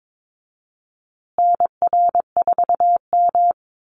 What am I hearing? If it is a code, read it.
Morse code, heard as a keyed tone, about 22 wpm, 700 Hz. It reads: NR4M
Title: Call Signs Repeated 2x - 22wpm